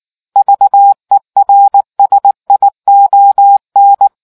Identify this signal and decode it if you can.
Morse code, heard as a keyed tone, about 19 wpm, 800 Hz. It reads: VERSION